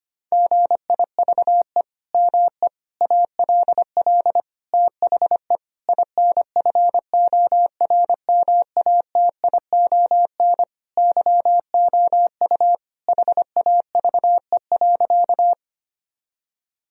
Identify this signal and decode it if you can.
Morse code, heard as a keyed tone, about 25 wpm, 700 Hz. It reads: GIVE ME ALL THE INFORMATION YOU HAVE.
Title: Sentences from Top 1000 Words - Review Part 3 - 25wpm